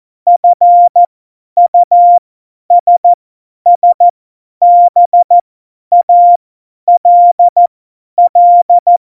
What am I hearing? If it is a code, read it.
Morse code, heard as a keyed tone, about 14 wpm, spaced as for 10 wpm, 700 Hz. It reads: FUSSBALL
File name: Morsekurs / Witze / Witz 22. Dezember 2020, 50 F70